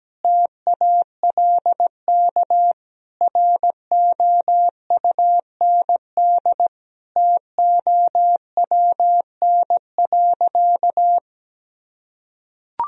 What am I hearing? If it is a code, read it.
Morse code, heard as a keyed tone, about 17 wpm, 700 Hz. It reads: TALK ROUND TOWN.
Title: Sentences from Top 1000 Words - Review Part 1 - 17wpm